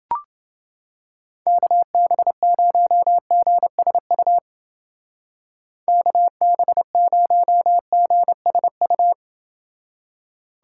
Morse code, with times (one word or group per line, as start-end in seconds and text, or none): 1.5-4.4 KB0GSU
5.9-9.1 KB0GSU